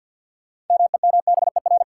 Morse code, heard as a keyed tone, about 52 wpm, 700 Hz. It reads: MEMBER
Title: Sets of 3 Words - New Word Drill - 50wpm